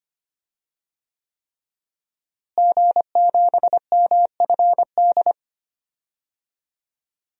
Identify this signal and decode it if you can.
Morse code, heard as a keyed tone, about 25 wpm, 700 Hz. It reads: G7MFD
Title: DXCC-100 Call Signs - 5 Characters - Repeated 3x - 25wpm